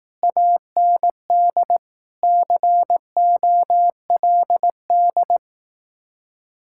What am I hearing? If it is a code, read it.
Morse code, heard as a keyed tone, about 18 wpm, 700 Hz. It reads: AND COLD
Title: Speed-Racing Common QSO Phrases Encore - 15wpm